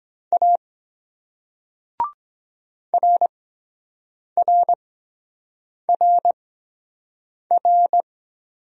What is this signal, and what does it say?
Morse code, heard as a keyed tone, about 23 wpm, spaced as for 7 wpm, 700 Hz. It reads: A RRRR